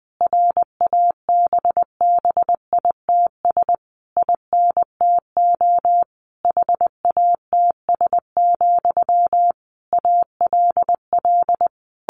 Morse code, heard as a keyed tone, about 20 wpm, 700 Hz. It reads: RABBITS INTO HATS, ALL
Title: A Sound Of Thunder 20wpm